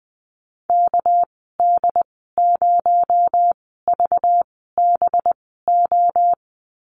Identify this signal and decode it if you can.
Morse code, heard as a keyed tone, about 20 wpm, spaced as for 15 wpm, 700 Hz. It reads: KD0VBO
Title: 219 Lesson 35g - Number 0 - Call Signs 20wpm